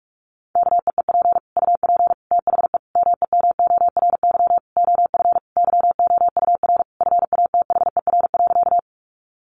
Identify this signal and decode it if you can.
Morse code, heard as a keyed tone, about 46 wpm, 700 Hz. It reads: KEEP UP THE MEMORY OF YOUR FATHER.
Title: Sentences from Top 1300 Words 45wpm